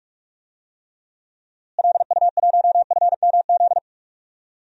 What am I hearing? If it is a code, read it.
Morse code, heard as a keyed tone, about 42 wpm, 700 Hz. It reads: LU1FMZ
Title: DXCC-100 Call Signs - 6 Characters - Repeated 3x - 45wpm